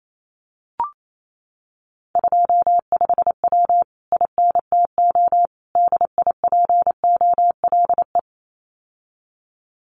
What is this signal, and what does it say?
Morse code, heard as a keyed tone, about 28 wpm, 700 Hz. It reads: 25W INTO DIPOLE
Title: Common QSO Phrases Encore 28wpm